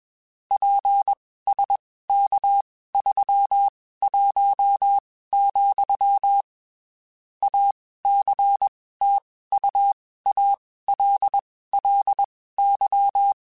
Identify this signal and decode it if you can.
Morse code, heard as a keyed tone, about 21 wpm, spaced as for 15 wpm, 800 Hz. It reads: PSK31, ACTUALLY
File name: (spaced) / QSOs Spaced-mp3s / WA7QOX 2 @20 sp